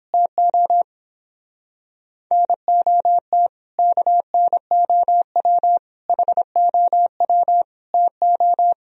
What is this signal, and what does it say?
Morse code, heard as a keyed tone, about 31 wpm, 700 Hz. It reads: TO NOT KNOW HOW TO